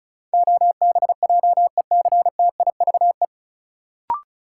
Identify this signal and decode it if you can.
Morse code, heard as a keyed tone, about 36 wpm, 700 Hz. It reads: OBJECTIVE
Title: Sentences from Top 1200 Words - New Word Drill Alt - 35wpm